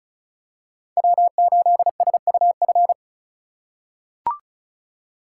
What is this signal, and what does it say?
Morse code, heard as a keyed tone, about 36 wpm, 700 Hz. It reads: W8SUF
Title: USA Call Signs - 3x Character Spacing - 35wpm